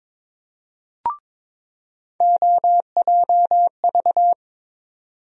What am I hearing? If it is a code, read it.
Morse code, heard as a keyed tone, about 22 wpm, 700 Hz. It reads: OJV
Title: USA Call Sign Postfixes - 3 Letters 22wpm